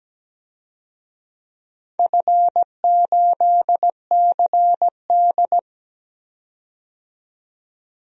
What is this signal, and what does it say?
Morse code, heard as a keyed tone, about 17 wpm, 700 Hz. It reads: F8CD
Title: DXCC-100 Call Signs - 4 Characters - 17wpm